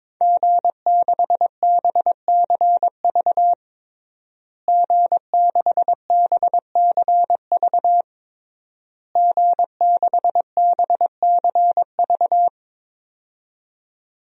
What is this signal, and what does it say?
Morse code, heard as a keyed tone, about 22 wpm, 700 Hz. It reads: G6BCV G6BCV G6BCV
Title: DXCC-100 Call Signs - 5 Characters - Repeated 3x - 22wpm